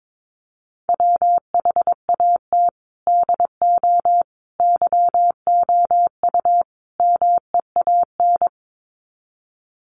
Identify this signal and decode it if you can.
Morse code, heard as a keyed tone, about 22 wpm, 700 Hz. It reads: WHAT DO YOU MEAN
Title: Sets of 4 Words 22wpm